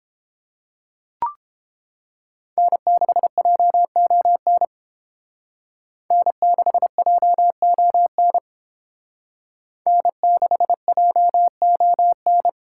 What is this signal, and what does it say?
Morse code, heard as a keyed tone, about 30 wpm, 700 Hz. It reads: N6JON N6JON N6JON